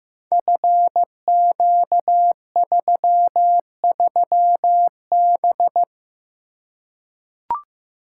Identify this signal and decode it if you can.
Morse code, heard as a keyed tone, about 15 wpm, 700 Hz. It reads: FQ33B
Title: Code Groups - 5 Letter:Numbers - 15wpm